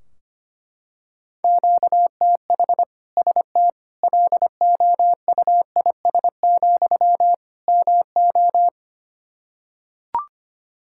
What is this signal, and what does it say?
Morse code, heard as a keyed tone, about 25 wpm, 700 Hz. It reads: QTH ST LOUIS, MO
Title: Practice QSO - Exchange 1 - Beginning - 25wpm